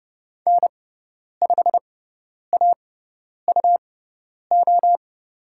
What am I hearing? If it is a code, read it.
Morse code, heard as a keyed tone, about 30 wpm, spaced as for 10 wpm, 700 Hz. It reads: N5AUO
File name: Lesson 18g - Letter m - Call Signs 30wpm